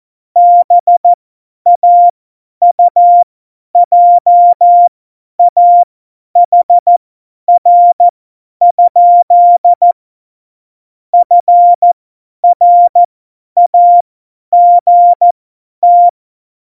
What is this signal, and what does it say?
Morse code, heard as a keyed tone, about 14 wpm, spaced as for 10 wpm, 700 Hz. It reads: BAUJAHR? FRAGT
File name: Morsekurs / Witze / Witz 19. September 17, Tempo 50